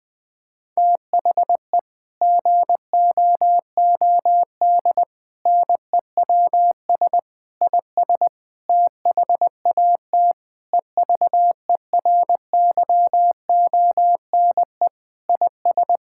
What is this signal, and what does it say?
Morse code, heard as a keyed tone, about 20 wpm, 700 Hz. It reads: THE GOOD NEWS IS THAT EVERYONE IS
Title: Sentences with 5-Word Phrases - Spoken Beginning - 20wpm